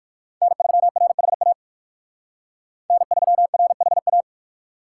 Morse code, heard as a keyed tone, about 59 wpm, 700 Hz. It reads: N3RHA N3RHA